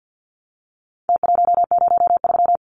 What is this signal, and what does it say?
Morse code, heard as a keyed tone, about 50 wpm, 700 Hz. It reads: T103
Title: Numbers - Contest Style - 4-Digit 50wpm